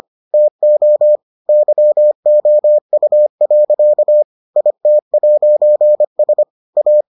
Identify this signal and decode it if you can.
Morse code, heard as a keyed tone, about 25 wpm, 600 Hz. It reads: TO YOU. IT'S A